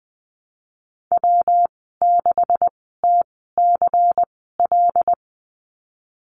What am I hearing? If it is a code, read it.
Morse code, heard as a keyed tone, about 20 wpm, spaced as for 14 wpm, 700 Hz. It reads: W6TCL
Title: 128 Lesson 21g - Number 6 - Call Signs 20wpm